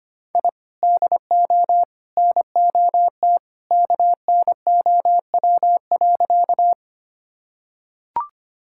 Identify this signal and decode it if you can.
Morse code, heard as a keyed tone, about 25 wpm, 700 Hz. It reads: I DO NOT KNOW.
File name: practice / 25wpm / Sentences with 3-Word Phrases - 25wpm